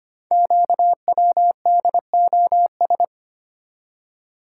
Morse code, heard as a keyed tone, about 25 wpm, 700 Hz. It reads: QWDOS
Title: Code Groups - 5 Letters - 25wpm